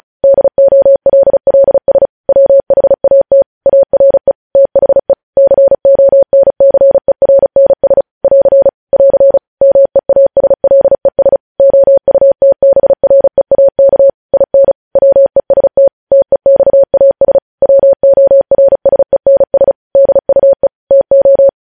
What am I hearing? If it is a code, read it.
Morse code, heard as a keyed tone, about 36 wpm, 550 Hz. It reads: DOLLS WHAT ARE THE CONCERNS <AR> <AR> MEASLES OUTBREAK IN WEST TEXAS WORSENS DUE TO
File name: podcast / News headlines-1x-35x35